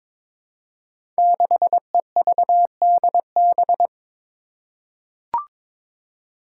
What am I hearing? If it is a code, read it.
Morse code, heard as a keyed tone, about 22 wpm, 700 Hz. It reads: 6EVDB
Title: Five Challenging Letter-Number Combinations 22wpm